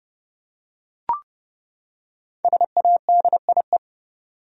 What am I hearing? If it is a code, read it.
Morse code, heard as a keyed tone, about 30 wpm, 700 Hz. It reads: SADIE
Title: USA Common Names - 30wpm